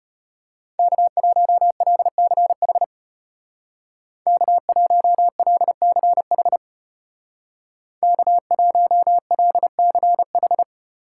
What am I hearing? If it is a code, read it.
Morse code, heard as a keyed tone, about 34 wpm, 700 Hz. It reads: K1LCH K1LCH K1LCH